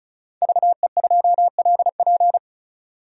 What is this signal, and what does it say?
Morse code, heard as a keyed tone, about 36 wpm, 700 Hz. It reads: VE2LP
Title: DXCC-100 Call Signs - 5 Characters - 35wpm